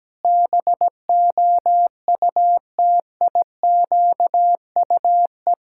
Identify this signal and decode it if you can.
Morse code, heard as a keyed tone, about 17 wpm, 700 Hz. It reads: BOUTIQUE
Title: Pangram Challenge v2 - 17wpm